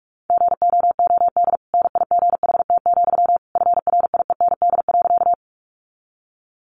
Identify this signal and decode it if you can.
Morse code, heard as a keyed tone, about 44 wpm, 700 Hz. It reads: GOOD NIGHT, FRIEND.
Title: Sentences from Top 300 Words 45wpm